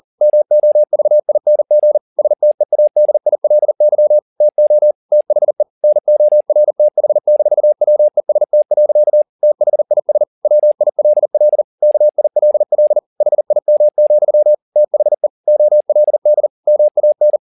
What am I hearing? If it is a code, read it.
Morse code, heard as a keyed tone, about 40 wpm, 600 Hz. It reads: MOVING STEADILY TO THE NORTH-WEST. THIS WILL KILL HIM, THE OLD MAN